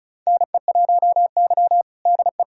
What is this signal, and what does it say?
Morse code, heard as a keyed tone, about 36 wpm, 700 Hz. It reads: NE1Y DE
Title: USA Callsign DE Callsign 35wpm